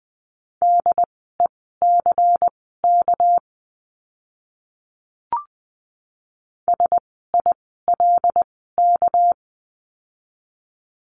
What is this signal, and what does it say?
Morse code, heard as a keyed tone, about 19 wpm, spaced as for 14 wpm, 700 Hz. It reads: DECK SILK
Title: 199 Lesson 32f - Letter k - Words 20wpm